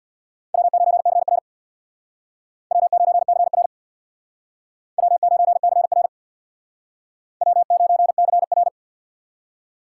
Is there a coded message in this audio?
Yes: Morse code from 0.5 to 8.7 s, heard as a keyed tone, about 64 wpm, 700 Hz. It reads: W9CR W9CR W9CR W9CR